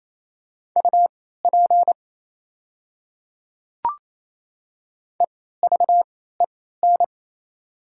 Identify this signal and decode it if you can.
Morse code, heard as a keyed tone, about 28 wpm, spaced as for 16 wpm, 700 Hz. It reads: UP EVEN
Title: 3x Character Spacing - Top 500 Words - 28wpm